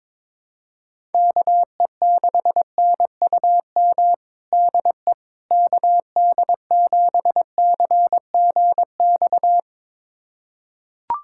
Morse code, heard as a keyed tone, about 22 wpm, 700 Hz. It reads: KE6NUM DE KD7CGX
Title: USA Callsign DE Callsign - 6 Characters - 22wpm